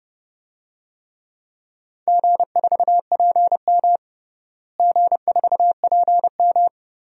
Morse code, heard as a keyed tone, about 28 wpm, 700 Hz. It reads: G4PM G4PM
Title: DXCC-100 Call Signs - 4 Characters - Repeated 3x - 30wpm